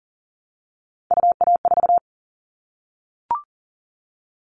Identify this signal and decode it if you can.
Morse code, heard as a keyed tone, about 40 wpm, 700 Hz. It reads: UA4
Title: Callsign Prefixes for Top 100 DXCC - 40wpm